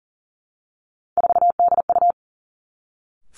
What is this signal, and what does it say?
Morse code, heard as a keyed tone, about 40 wpm, 700 Hz. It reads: 4DU
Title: Three Challenging Letter-Number Combinations 40wpm